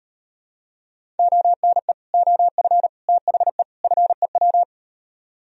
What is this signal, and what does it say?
Morse code, heard as a keyed tone, about 36 wpm, 700 Hz. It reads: ONE OF THE FEW